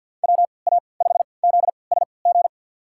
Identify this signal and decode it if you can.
Morse code, heard as a keyed tone, about 53 wpm, spaced as for 29 wpm, 700 Hz. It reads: WA5ZSG
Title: USA Call Signs - 6 Characters - 3x Character Spacing - 50wpm